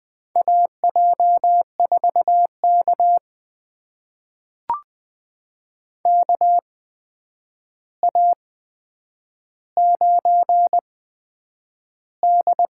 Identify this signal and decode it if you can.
Morse code, heard as a keyed tone, about 20 wpm, spaced as for 9 wpm, 700 Hz. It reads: AJ4K KA9D
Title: USA Call Signs - 4 Characters - 8x Character Spacing - 20wpm